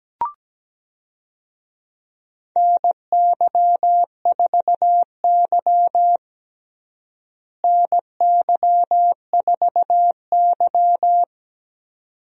Morse code, heard as a keyed tone, about 17 wpm, 700 Hz. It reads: NY4Y NY4Y
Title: USA Call Signs - 4 Characters - Repeated 2x - 17wpm